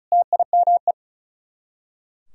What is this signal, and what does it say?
Morse code, heard as a keyed tone, about 44 wpm, 700 Hz. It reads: TIME